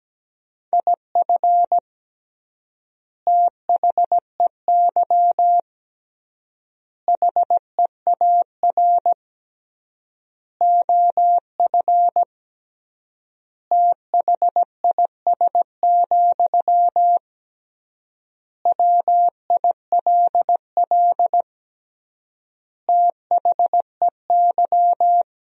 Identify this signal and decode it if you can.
Morse code, heard as a keyed tone, about 17 wpm, 700 Hz. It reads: IF THEY HEAR OF THIS, WILL THEY
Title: 3x Word-Spacing - Sentences from Top 400 Words - 17wpm